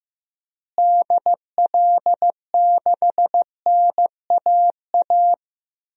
Morse code, heard as a keyed tone, about 15 wpm, 700 Hz. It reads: DL6NAA
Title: DXCC-100 Call Signs - 6 Characters - Repeated 3x - 15wpm